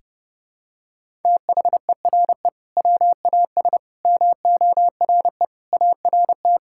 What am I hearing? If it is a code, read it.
Morse code, heard as a keyed tone, about 30 wpm, 700 Hz. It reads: THERE WAS MORE ART
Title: Sentences from Top 1000 Words - Review Part 6 - 30wpm